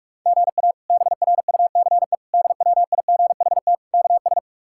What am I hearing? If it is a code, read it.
Morse code, heard as a keyed tone, about 44 wpm, 700 Hz. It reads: GA BRUCE DWIGHT KS
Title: K1USN SST QSO Practice - 45wpm